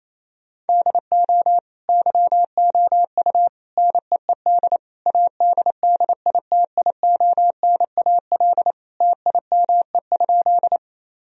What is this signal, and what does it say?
Morse code, heard as a keyed tone, about 28 wpm, 700 Hz. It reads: DO YOU NEED ADDITIONAL TIME?
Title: Sentences from Top 1000 Words - Review Part 4 - 28wpm